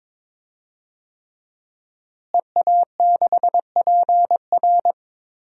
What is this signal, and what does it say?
Morse code, heard as a keyed tone, about 22 wpm, 700 Hz. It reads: EA6PR
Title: DXCC-100 Call Signs - 5 Characters - Repeated 3x - 22wpm